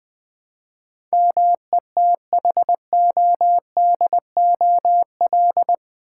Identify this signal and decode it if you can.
Morse code, heard as a keyed tone, about 21 wpm, 700 Hz. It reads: METHODOL
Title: Build A Word - 4 Letters At A Time - 11-Letter Words - 20wpm